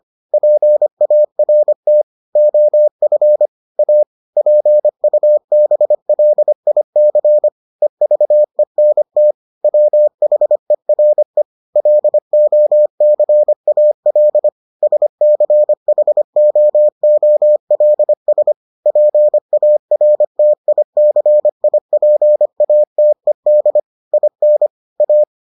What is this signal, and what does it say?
Morse code, heard as a keyed tone, about 25 wpm, 600 Hz. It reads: PART OF A PUBLIC EVENT WHERE LOCAL SCHOOLS PARTICIPATED IN A